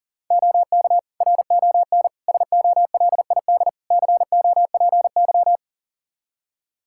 Morse code, heard as a keyed tone, about 37 wpm, 690 Hz. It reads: OK RON SOLID COPY
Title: Practice QSO - Exchange 3 - More Info - 40wpm